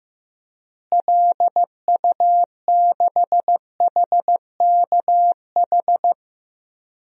Morse code, heard as a keyed tone, about 15 wpm, 700 Hz. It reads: LU6HKH
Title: DXCC-100 Call Signs - 6 Characters - Repeated 3x - 15wpm